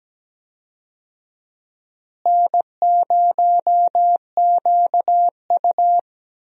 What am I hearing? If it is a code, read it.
Morse code, heard as a keyed tone, about 17 wpm, 700 Hz. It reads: N0QU
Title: USA Call Signs - 4 Characters - Repeated 2x - 17wpm